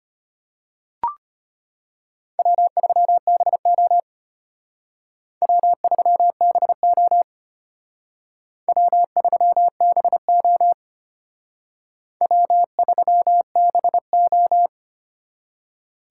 Morse code, heard as a keyed tone, about 32 wpm, 700 Hz. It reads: W3BO W3BO W3BO W3BO